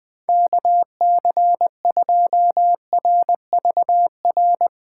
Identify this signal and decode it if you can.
Morse code, heard as a keyed tone, about 20 wpm, 700 Hz. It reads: KC2RVR
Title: Speed Racing - USA Call Signs 17wpm